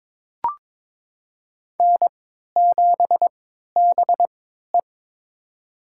Morse code, heard as a keyed tone, about 22 wpm, spaced as for 13 wpm, 700 Hz. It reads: N7BE
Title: USA Call Signs - 4 Characters - 3x Character Spacing - 22wpm